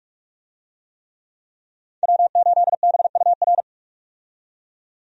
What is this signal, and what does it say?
Morse code, heard as a keyed tone, about 43 wpm, 700 Hz. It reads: W8BUR
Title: USA Call Signs - 5 Characters - 45wpm